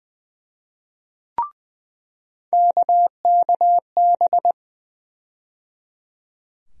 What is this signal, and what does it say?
Morse code, heard as a keyed tone, about 21 wpm, 700 Hz. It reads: KKB
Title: USA Call Sign Postfixes - 3 Letters 20wpm